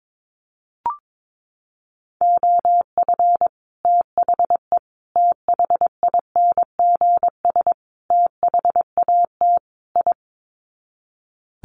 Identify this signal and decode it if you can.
Morse code, heard as a keyed tone, about 22 wpm, 700 Hz. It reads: OF THE THINGS THAT I
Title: Sets of 5 Words Encore 22wpm